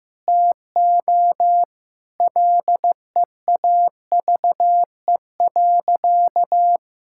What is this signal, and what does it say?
Morse code, heard as a keyed tone, about 15 wpm, 700 Hz. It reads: TO LEAVE.
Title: Sentences from Top 1000 Words - Review Part 5 - 15wpm